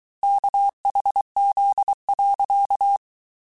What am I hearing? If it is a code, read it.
Morse code, heard as a keyed tone, about 23 wpm, 800 Hz. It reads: KHZ.
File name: cours-du-mercredi / audios / cours-855-texte-chiffre-vitesse-22-mpm